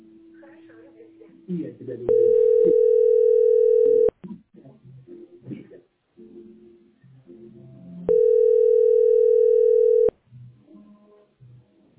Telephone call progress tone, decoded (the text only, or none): ringback tone